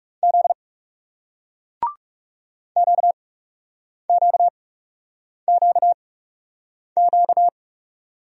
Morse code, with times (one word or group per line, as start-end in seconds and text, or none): 0.2-0.5 Z
2.8-7.5 QQQQ